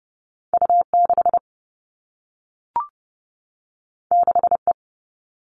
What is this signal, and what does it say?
Morse code, heard as a keyed tone, about 30 wpm, 700 Hz. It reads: U6 6E
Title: All Cut Numbers - 2-Digit 30wpm